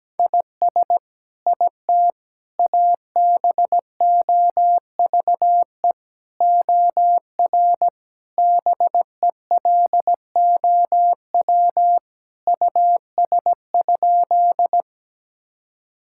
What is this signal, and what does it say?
Morse code, heard as a keyed tone, about 17 wpm, 700 Hz. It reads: IS IT ABOVE OR BELOW US?
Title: Sentences from Top 1000 Words - Review Part 5 - 17wpm